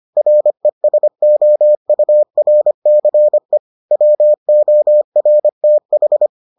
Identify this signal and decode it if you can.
Morse code, heard as a keyed tone, about 25 wpm, 600 Hz. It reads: RESOURCE WORTH